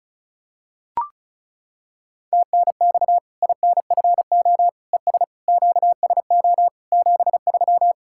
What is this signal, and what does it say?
Morse code, heard as a keyed tone, about 34 wpm, 700 Hz. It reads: TNX INFO ES QSO 73
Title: Practice QSO - Exchange 4 - Ending - 35wpm